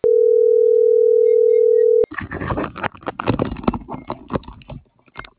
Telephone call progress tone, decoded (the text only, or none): ringback tone